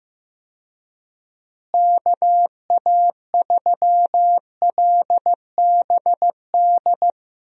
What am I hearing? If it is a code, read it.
Morse code, heard as a keyed tone, about 15 wpm, 700 Hz. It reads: KA3LBD
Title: USA Call Signs - 6 Characters - Repeated 2x - 15wpm